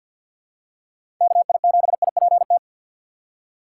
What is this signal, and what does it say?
Morse code, heard as a keyed tone, about 51 wpm, 700 Hz. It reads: KI7IPT
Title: USA Call Signs - 6 Characters - 6x Character Spacing - 50wpm